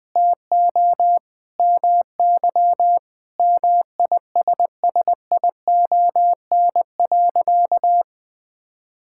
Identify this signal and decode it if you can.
Morse code, heard as a keyed tone, about 21 wpm, 700 Hz. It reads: TO MY MISSION.
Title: Sentences from Top 1600 Words 20wpm